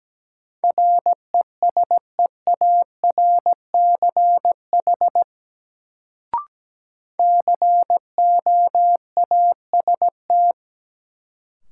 Morse code, heard as a keyed tone, about 17 wpm, 700 Hz. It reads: RESEARCH COAST
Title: Sentences from Top 1000 Words - Review Part 4 - New Word Drill - 17wpm